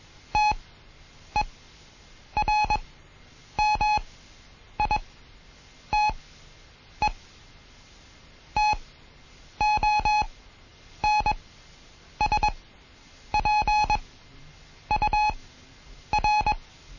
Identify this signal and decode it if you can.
Morse code, heard as a keyed tone, about 21 wpm, spaced as for 8 wpm, 840 Hz. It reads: TERMITE TONSPUR
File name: Morsen / Kurs A 20